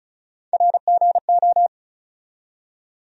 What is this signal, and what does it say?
Morse code, heard as a keyed tone, about 36 wpm, 700 Hz. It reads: RGO
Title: USA Call Sign Postfixes - 3 Letters 35wpm